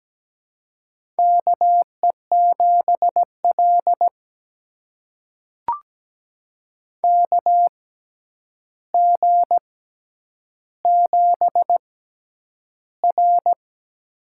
Morse code, heard as a keyed tone, about 17 wpm, spaced as for 9 wpm, 700 Hz. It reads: KE7L KG7R
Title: USA Call Signs - 6x Character Spacing - 17wpm